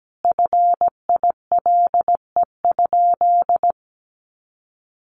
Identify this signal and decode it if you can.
Morse code, heard as a keyed tone, about 17 wpm, 700 Hz. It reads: FILE?
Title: Sentences from Top 1200 Words 17wpm